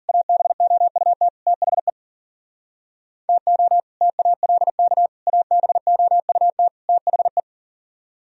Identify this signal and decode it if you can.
Morse code, heard as a keyed tone, about 42 wpm, 700 Hz. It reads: ABOUT THE TO TALK ABOUT THE
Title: Speed-Racing - Sets of 4 Words 40wpm